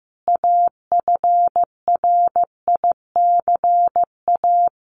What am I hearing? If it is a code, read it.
Morse code, heard as a keyed tone, about 15 wpm, 700 Hz. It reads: AFRICA
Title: A Sound Of Thunder 15wpm